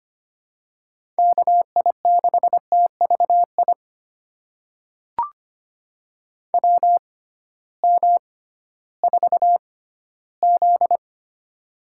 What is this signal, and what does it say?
Morse code, heard as a keyed tone, about 25 wpm, spaced as for 14 wpm, 700 Hz. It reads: KI6TVI WM4Z
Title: USA Call Signs - 6x Character Spacing - 25wpm